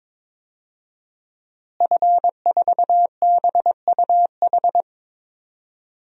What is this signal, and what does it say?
Morse code, heard as a keyed tone, about 22 wpm, 700 Hz. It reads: F4BUH
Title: DXCC-100 Call Signs - 5 Characters - Repeated 3x - 22wpm